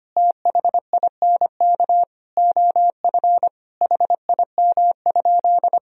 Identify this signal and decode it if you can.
Morse code, heard as a keyed tone, about 25 wpm, 700 Hz. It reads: THINK OF HIM?
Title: Sentences from Top 200 Words and Abbreviations 25wpm